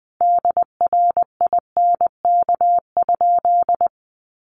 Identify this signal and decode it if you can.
Morse code, heard as a keyed tone, about 20 wpm, 700 Hz. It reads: DRINK?
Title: Sentences from Top 1500 Words 20wpm